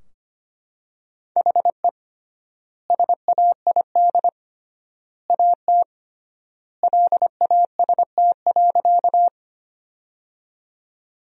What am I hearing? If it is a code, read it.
Morse code, heard as a keyed tone, about 25 wpm, 700 Hz. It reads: HE SAID AT LAST.
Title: 3x Word-Spacing - Sentences from Top 300 Words - 25wpm